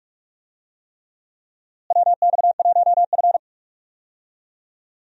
Morse code, heard as a keyed tone, about 46 wpm, 700 Hz. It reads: WX1F
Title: USA Call Signs - 4 Characters - Repeated 3x - 45wpm